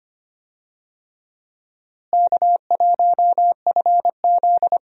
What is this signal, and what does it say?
Morse code, heard as a keyed tone, about 25 wpm, 700 Hz. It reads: K1FZ
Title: USA Call Signs - 4 Characters - Repeated 3x - 25wpm